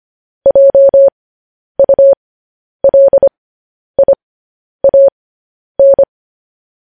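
Morse code, heard as a keyed tone, about 25 wpm, spaced as for 10 wpm, 550 Hz. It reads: JULIAN
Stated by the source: Morse code news headlines sent at 10 wpm for practice.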